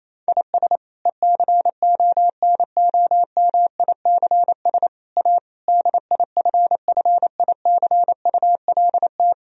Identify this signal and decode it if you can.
Morse code, heard as a keyed tone, about 28 wpm, 700 Hz. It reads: IS ECONOMICS A DIFFICULT
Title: Sentences from Top 1000 Words - Review Part 3 - 28wpm